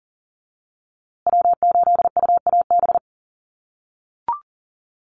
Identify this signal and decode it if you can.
Morse code, heard as a keyed tone, about 40 wpm, 700 Hz. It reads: W8UAB
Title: USA Call Signs Repeated 2x 40wpm